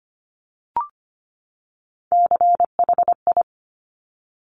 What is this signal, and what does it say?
Morse code, heard as a keyed tone, about 25 wpm, 700 Hz. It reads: CHI
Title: Top Three Letter Combinations in Words 25wpm